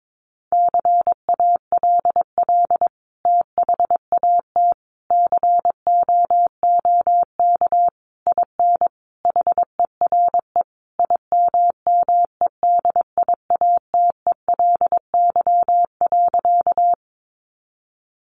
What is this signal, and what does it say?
Morse code, heard as a keyed tone, about 22 wpm, 700 Hz. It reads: CALL THAT COOK IN HERE IMMEDIATELY.
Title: Sentences from Top 1100 Words 22wpm